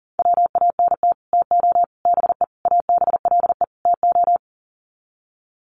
Morse code, heard as a keyed tone, about 40 wpm, 700 Hz. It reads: WANT TO BE ABLE TO
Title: Sets of 5 Words 40wpm